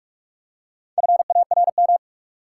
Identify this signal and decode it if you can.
Morse code, heard as a keyed tone, about 47 wpm, 700 Hz. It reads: FARM
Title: Four Letter Words - 45wpm